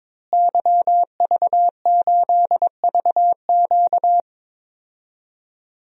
Morse code, heard as a keyed tone, about 22 wpm, 700 Hz. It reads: YV8VQ
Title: DXCC-100 Call Signs - 5 Characters - Repeated 2x - 22wpm